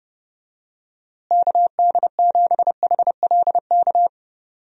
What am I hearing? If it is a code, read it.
Morse code, heard as a keyed tone, about 30 wpm, 700 Hz. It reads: KD7HLK
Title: USA Call Signs - 6 Characters - Repeated 2x - 30wpm